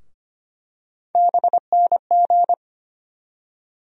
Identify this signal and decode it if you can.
Morse code, heard as a keyed tone, about 25 wpm, 700 Hz. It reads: BNG
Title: USA Call Sign Postfixes - 3 Letters 25wpm